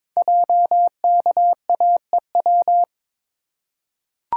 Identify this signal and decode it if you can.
Morse code, heard as a keyed tone, about 22 wpm, 700 Hz. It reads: JKAEW
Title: Code Groups - 5 Letters - 22wpm